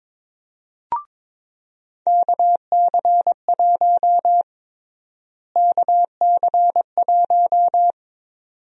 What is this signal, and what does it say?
Morse code, heard as a keyed tone, about 22 wpm, 700 Hz. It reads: KC1 KC1
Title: USA Call Sign Prefixes - Repeated 2x - 22wpm